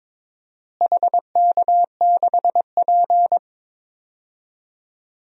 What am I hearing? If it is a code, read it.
Morse code, heard as a keyed tone, about 22 wpm, 700 Hz. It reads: HK6P
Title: DXCC-100 Call Signs - 4 Characters - Repeated 3x - 22wpm